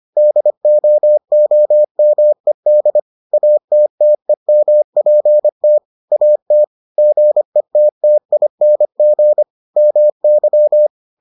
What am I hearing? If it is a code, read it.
Morse code, heard as a keyed tone, about 25 wpm, 600 Hz. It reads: DOOMED ATTEMPT AT GETTING MY